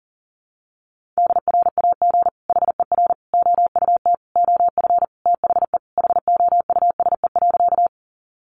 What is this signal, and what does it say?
Morse code, heard as a keyed tone, about 40 wpm, 700 Hz. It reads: DRAG HER OUT OF THE HOUSE.
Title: Sentences from Top 1500 Words 40wpm